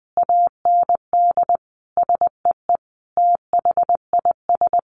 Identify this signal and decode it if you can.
Morse code, heard as a keyed tone, about 20 wpm, 700 Hz. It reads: AND SEE THIS
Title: Sentences from Top 1100 Words 20wpm